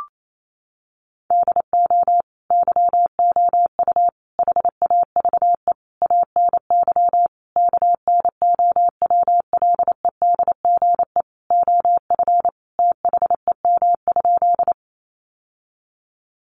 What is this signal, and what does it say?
Morse code, heard as a keyed tone, about 28 wpm, 700 Hz. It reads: DO YOU HAVE ANY KNOWLEDGE OF THEM?
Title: Sentences from Top 600 Words 28wpm